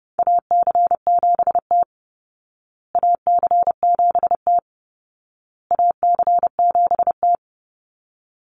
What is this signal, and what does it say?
Morse code, heard as a keyed tone, about 30 wpm, 700 Hz. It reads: AC7T AC7T AC7T
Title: Call Signs Repeated 2x - 30wpm